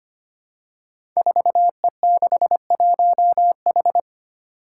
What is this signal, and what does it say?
Morse code, heard as a keyed tone, about 25 wpm, 700 Hz. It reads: 4E61H
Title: Five Challenging Letter-Number Combinations 25wpm